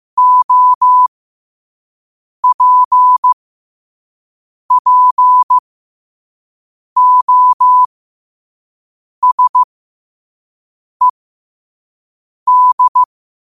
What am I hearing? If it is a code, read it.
Morse code, heard as a keyed tone, about 15 wpm, spaced as for 5 wpm, 1000 Hz. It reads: OPPOSED